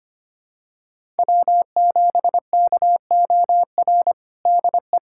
Morse code, heard as a keyed tone, about 25 wpm, 700 Hz. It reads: W7KOR DE
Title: USA Callsign DE Callsign 25wpm